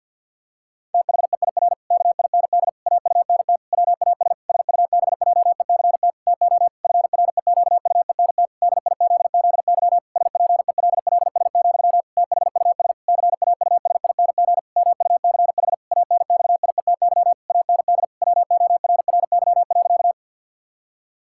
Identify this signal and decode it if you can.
Morse code, heard as a keyed tone, about 50 wpm, 700 Hz. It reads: THEIR KIND AUNT WAS SUBJECT TO FREQUENT DIZZY SPELLS, THUS CAUSING MUCH ANXIETY AND WORRY.
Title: Pangram Challenge v2 - 50wpm